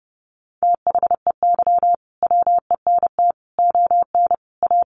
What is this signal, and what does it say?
Morse code, heard as a keyed tone, about 30 wpm, 700 Hz. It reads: THEY WENT ON A
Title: Sentences from Top 1400 Words 30wpm